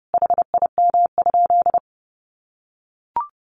Morse code, heard as a keyed tone, about 30 wpm, 700 Hz. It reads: HIM?
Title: Sentences from Top 100 Words 30wpm